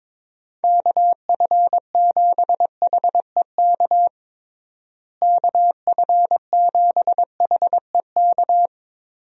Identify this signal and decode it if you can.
Morse code, heard as a keyed tone, about 22 wpm, 700 Hz. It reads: KF7HEK KF7HEK
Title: USA Call Signs - 6 Characters - Repeated 3x - 22wpm